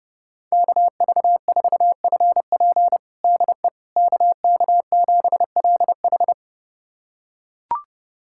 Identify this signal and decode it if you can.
Morse code, heard as a keyed tone, about 30 wpm, 700 Hz. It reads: KV4FP DE KK7LH
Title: USA Callsign DE Callsign - 5 Characters - 30wpm